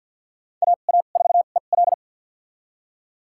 Morse code, heard as a keyed tone, about 46 wpm, spaced as for 35 wpm, 690 Hz. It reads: AA4EL
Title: USA Call Signs - 2x Character Spacing - 50wpm